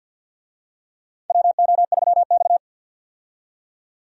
Morse code, heard as a keyed tone, about 48 wpm, 700 Hz. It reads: WO3X
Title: USA Call Signs - 4 Characters - 50wpm